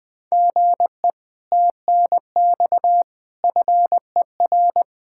Morse code, heard as a keyed tone, about 20 wpm, 700 Hz. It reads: GE TNX FER
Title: Speed-Racing Common QSO Phrases Encore - 17wpm